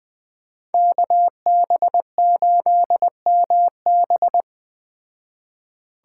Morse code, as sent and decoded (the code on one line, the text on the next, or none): -.- -... ---.. -- -...
KB8MB